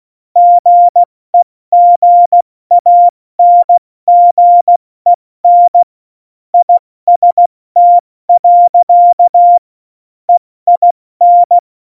Morse code, heard as a keyed tone, about 16 wpm, 700 Hz. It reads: GEGANGEN IST. EIN
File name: Morsekurs / Witze / Witz 29. Januar 2019, 70 F80